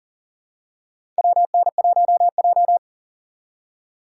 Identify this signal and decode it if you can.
Morse code, heard as a keyed tone, about 40 wpm, 700 Hz. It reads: WN1J
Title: USA Call Signs - 4 Characters - 2x Character Spacing - 40wpm